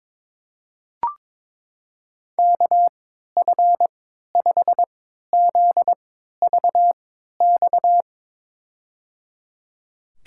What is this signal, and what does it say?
Morse code, heard as a keyed tone, about 22 wpm, spaced as for 12 wpm, 700 Hz. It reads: KF5ZVX
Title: USA Call Signs - 6 Characters - 3x Character Spacing - 22wpm